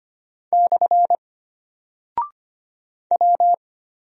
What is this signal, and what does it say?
Morse code, heard as a keyed tone, about 25 wpm, 700 Hz. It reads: / W
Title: Single Character (Letter:Number:Punctuation:Prosign) Alt 25wpm